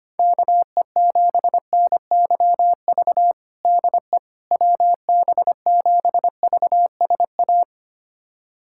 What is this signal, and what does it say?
Morse code, heard as a keyed tone, about 25 wpm, 700 Hz. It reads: KE7NYV DE WB7VSA
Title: USA Callsign DE Callsign 25wpm